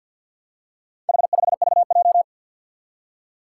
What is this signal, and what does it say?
Morse code, heard as a keyed tone, about 50 wpm, 700 Hz. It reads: H5VJ
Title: Four Challenging Letter-Number Combinations 50wpm